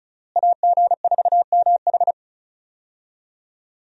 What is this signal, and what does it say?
Morse code, heard as a keyed tone, about 34 wpm, 700 Hz. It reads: AG4MH
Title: USA Call Signs - 5 Characters - Repeated 3x - 35wpm